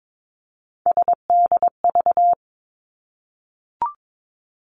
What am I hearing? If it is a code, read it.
Morse code, heard as a keyed tone, about 22 wpm, 700 Hz. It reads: SDV
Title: Three Challenging Letter-Number Combinations 22wpm